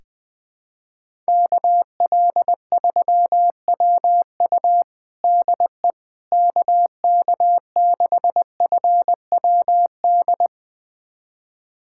Morse code, heard as a keyed tone, about 20 wpm, 700 Hz. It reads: KL3WU DE KK6FWD
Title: USA Callsign DE Callsign 20wpm